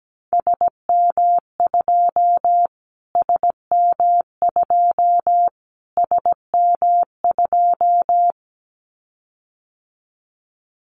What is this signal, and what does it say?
Morse code, heard as a keyed tone, about 17 wpm, 700 Hz. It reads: SM2 SM2 SM2
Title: Callsign Prefixes for Top 25 DXCC - Repeated 3x - 17wpm